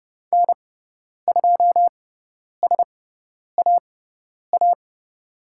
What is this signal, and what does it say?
Morse code, heard as a keyed tone, about 30 wpm, spaced as for 10 wpm, 700 Hz. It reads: N2SAA
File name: Lesson 18g - Letter m - Call Signs 30wpm